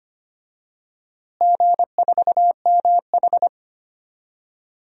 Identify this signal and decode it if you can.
Morse code, heard as a keyed tone, about 25 wpm, 700 Hz. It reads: G4MH
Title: DXCC-100 Call Signs - 4 Characters - Repeated 3x - 25wpm